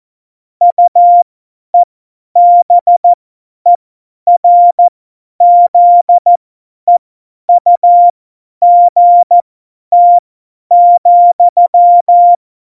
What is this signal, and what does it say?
Morse code, heard as a keyed tone, about 14 wpm, spaced as for 10 wpm, 700 Hz. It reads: UEBERZEUGT,
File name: Morsekurs / Witze / Witz 26. Januar 2021, 50 F70